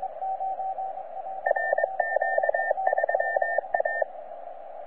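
Morse code, heard as a keyed tone, about 22 wpm, 1800 Hz, with radio noise. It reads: RQ3A
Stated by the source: C91RF DXpedition station heard on 10 m CW, from DX World.net.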